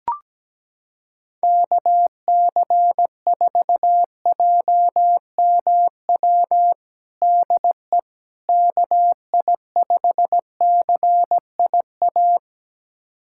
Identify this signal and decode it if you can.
Morse code, heard as a keyed tone, about 17 wpm, 700 Hz. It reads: KC4JMW DE KI5CIA
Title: USA Callsign DE Callsign 17wpm